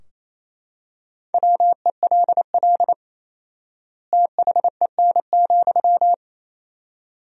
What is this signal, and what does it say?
Morse code, heard as a keyed tone, about 28 wpm, 700 Hz. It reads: WELL THEN,
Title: 4x Word-Spacing - Sentences from Top 200 Words - 28wpm